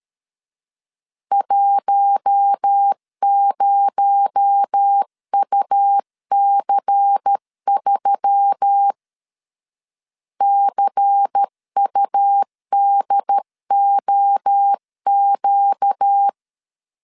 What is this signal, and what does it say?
Morse code, heard as a keyed tone, about 13 wpm, 800 Hz. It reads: 10UC3 CUDOQ
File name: LEZIONI / 7 GF9Z / 7 GF9Z 12wpm 2147